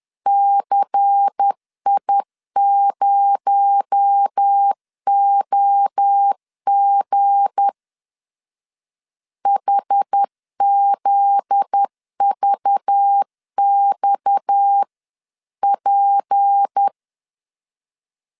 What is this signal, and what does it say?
Morse code, heard as a keyed tone, about 10 wpm, 800 Hz. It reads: CI0OG HZVXP
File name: LEZIONI / 7 GF9Z / 7 GF9Z 10wpm 2611